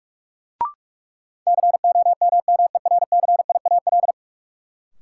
Morse code, heard as a keyed tone, about 46 wpm, 700 Hz. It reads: COMMERCIAL
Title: Sentences from Top 1000 Words - Review Part 5 - New Word Drill - 45wpm